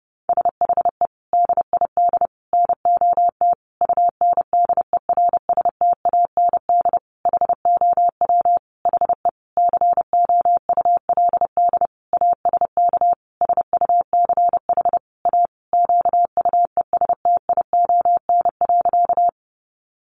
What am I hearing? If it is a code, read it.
Morse code, heard as a keyed tone, about 30 wpm, 700 Hz. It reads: SHE DID NOT UNDERSTAND HOW HE COULD ASK SUCH A QUESTION.
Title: Sentences from Top 600 Words 30wpm